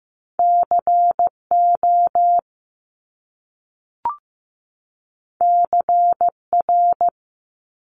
Morse code, heard as a keyed tone, about 15 wpm, 700 Hz. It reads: CO CR
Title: Top Two Letter Combinations in Words 15wpm